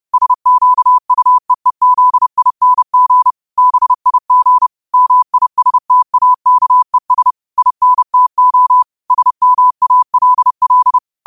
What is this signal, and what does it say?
Morse code, heard as a keyed tone, about 30 wpm, 1000 Hz. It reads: SQUEEZING BIG MISTAKES INTO SMALL
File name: qotdfiles / 30wpm